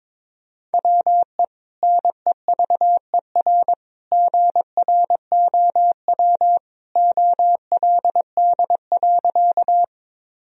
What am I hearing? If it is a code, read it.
Morse code, heard as a keyed tone, about 22 wpm, 700 Hz. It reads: WE NEVER GROW OLD.
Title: Sentences from Top 1000 Words - Review Part 1 - 22wpm